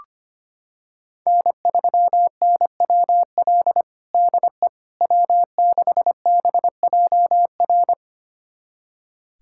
Morse code, heard as a keyed tone, about 25 wpm, 700 Hz. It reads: N3NWL DE W6BJR
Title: USA Callsign DE Callsign - 5 Characters - 25wpm